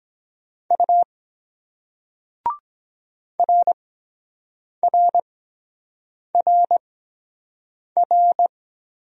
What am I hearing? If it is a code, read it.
Morse code, heard as a keyed tone, about 23 wpm, spaced as for 7 wpm, 700 Hz. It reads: U RRRR